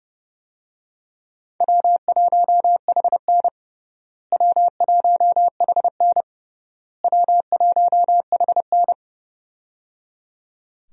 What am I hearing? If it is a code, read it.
Morse code, heard as a keyed tone, about 30 wpm, 700 Hz. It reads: W1HN W1HN W1HN
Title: USA Call Signs - 4 Characters - Repeated 3x - 30wpm